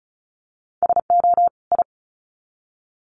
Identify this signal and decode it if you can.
Morse code, heard as a keyed tone, about 34 wpm, 700 Hz. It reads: SO I
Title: Sets of 2 Words 35wpm